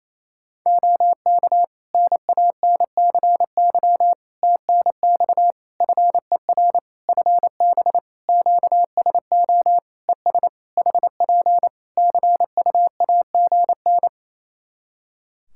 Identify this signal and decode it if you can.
Morse code, heard as a keyed tone, about 28 wpm, 700 Hz. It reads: OK NANCY TNX FER FB QSO ES HP CUAGN
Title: Practice QSO - Exchange 4 - Ending - 28wpm